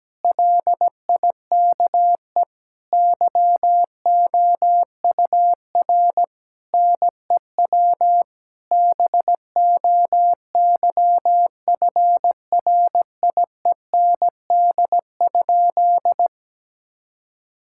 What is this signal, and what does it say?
Morse code, heard as a keyed tone, about 17 wpm, 700 Hz. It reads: LIKE YOUR NEW BOYFRIEND?
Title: Sentences from Top 1000 Words - Review Part 5 - 17wpm